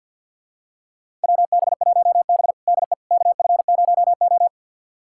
Morse code, heard as a keyed tone, about 53 wpm, 700 Hz. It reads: WB1B DE KF0O
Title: USA Callsign DE Callsign - 4 Characters - 50wpm